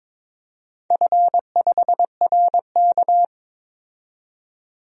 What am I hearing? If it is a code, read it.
Morse code, heard as a keyed tone, about 22 wpm, 700 Hz. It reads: F5RK
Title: DXCC-100 Call Signs - 4 Characters - 22wpm